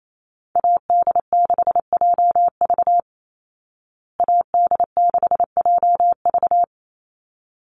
Morse code, heard as a keyed tone, about 28 wpm, 700 Hz. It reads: AD6JV AD6JV
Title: Call Signs Repeated 3x - 28wpm